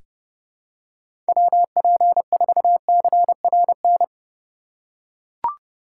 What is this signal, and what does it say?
Morse code, heard as a keyed tone, about 31 wpm, 700 Hz. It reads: WP4CRN
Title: USA Call Signs - 6 Characters - 3x Character Spacing - 30wpm